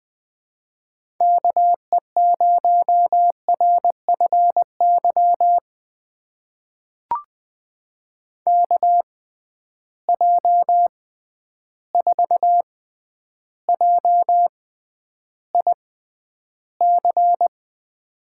Morse code, heard as a keyed tone, about 20 wpm, spaced as for 10 wpm, 700 Hz. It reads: KE0RFY KJ4JIC
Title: USA Call Signs - 6 Characters - 6x Character Spacing - 20wpm